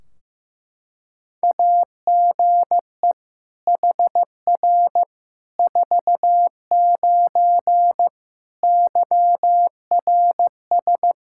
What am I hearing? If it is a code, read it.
Morse code, heard as a keyed tone, about 15 wpm, 700 Hz. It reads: AGE HR 49 YRS
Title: Practice QSO - Exchange 3 - More Info - 15wpm